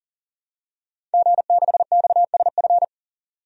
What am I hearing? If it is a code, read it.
Morse code, heard as a keyed tone, about 40 wpm, 700 Hz. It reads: G6XSF
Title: DXCC-100 Call Signs - 5 Characters - 40wpm